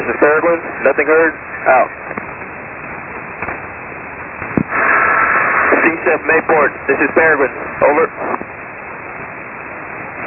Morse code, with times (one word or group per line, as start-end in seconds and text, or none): none